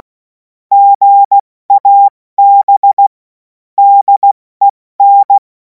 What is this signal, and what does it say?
Morse code, heard as a keyed tone, about 16 wpm, 800 Hz. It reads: GAB DEN
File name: Morsekurs / Witze / Witz 31. März 2026 70 F80